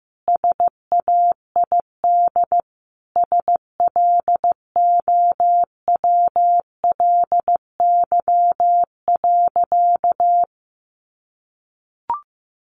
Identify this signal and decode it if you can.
Morse code, heard as a keyed tone, about 15 wpm, 700 Hz. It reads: SAID SLOWLY.
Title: Sentences from Top 1100 Words 15wpm